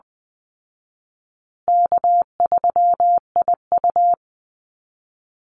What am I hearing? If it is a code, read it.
Morse code, heard as a keyed tone, about 20 wpm, 700 Hz. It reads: K3IU
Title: Easy Call Signs Repeated 3x - 20wpm